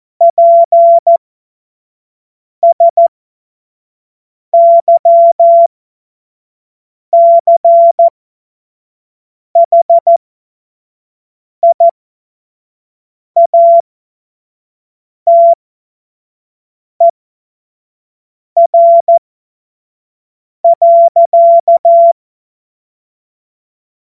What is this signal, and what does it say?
Morse code, heard as a keyed tone, about 14 wpm, spaced as for 5 wpm, 680 Hz. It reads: PSYCHIATER.
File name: Morsekurs / Witze / Witz 05 März 2024 25 F70